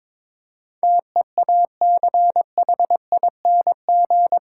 Morse code, heard as a keyed tone, about 21 wpm, 700 Hz. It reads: TEACHING
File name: practice / 22wpm / Eight Letter Words - 22wpm